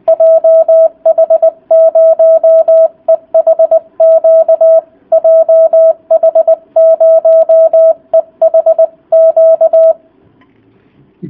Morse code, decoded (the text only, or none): JH0EHQ JH0EHQ